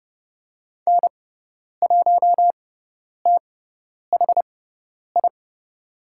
Morse code, heard as a keyed tone, about 30 wpm, spaced as for 10 wpm, 700 Hz. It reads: N1THI
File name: Lesson 16g - Letter c - Call Signs 30wpm